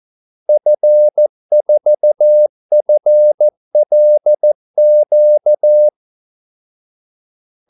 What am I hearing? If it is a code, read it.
Morse code, heard as a keyed tone, about 14 wpm, 600 Hz. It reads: F4FLQ
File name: Callsigns-14wpm